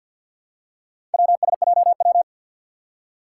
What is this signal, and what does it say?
Morse code, heard as a keyed tone, about 51 wpm, 700 Hz. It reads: WSJW
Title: Four Challenging Letter-Number Combinations 50wpm